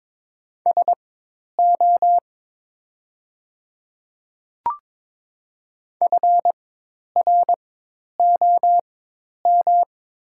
Morse code, heard as a keyed tone, about 22 wpm, spaced as for 10 wpm, 700 Hz. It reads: SO FROM
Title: 4x Character Spacing - Top 100 Words - 22wpm